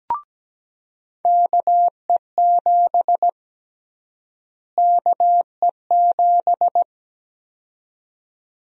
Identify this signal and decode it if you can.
Morse code, heard as a keyed tone, about 17 wpm, 700 Hz. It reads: KE7 KE7
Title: USA Call Sign Prefixes - Repeated 2x - 17wpm